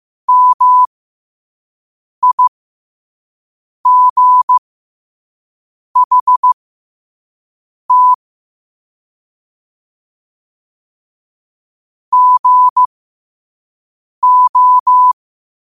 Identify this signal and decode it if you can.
Morse code, heard as a keyed tone, about 15 wpm, spaced as for 5 wpm, 1000 Hz. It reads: MIGHT GO
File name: qotdfiles / 5wpm